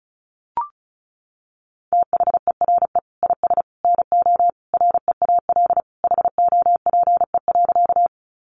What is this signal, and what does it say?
Morse code, heard as a keyed tone, about 34 wpm, 700 Hz. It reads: THERE IS NO REAL HOPE.
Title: Sentences from Top 1000 Words 35wpm